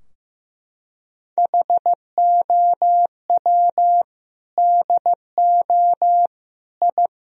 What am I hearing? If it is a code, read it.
Morse code, heard as a keyed tone, about 16 wpm, 700 Hz. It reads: HOW DO I
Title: Sentences from Top 1000 Words - Review Part 2 - 15wpm